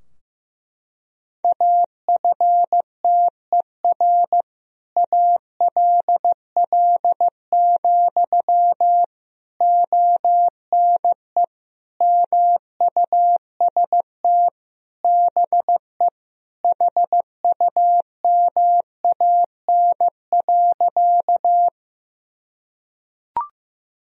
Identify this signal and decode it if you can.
Morse code, heard as a keyed tone, about 15 wpm, 700 Hz. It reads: AFTER ALL, ONE MUST BE HUMAN.
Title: Sentences from Top 1000 Words - Review Part 3 - 15wpm